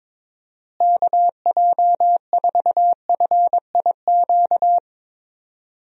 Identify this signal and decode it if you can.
Morse code, heard as a keyed tone, about 22 wpm, 700 Hz. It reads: KJ4FIQ
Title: USA Call Signs - 6 Characters - Repeated 3x - 22wpm